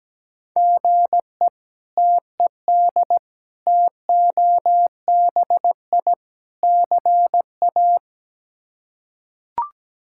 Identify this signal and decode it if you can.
Morse code, heard as a keyed tone, about 17 wpm, 700 Hz. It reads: GE TED TOBI CA
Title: K1USN SST QSO Practice - 17wpm